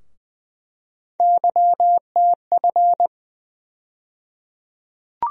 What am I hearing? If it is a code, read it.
Morse code, heard as a keyed tone, about 21 wpm, 700 Hz. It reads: YTF
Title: USA Call Sign Postfixes - 3 Letters 20wpm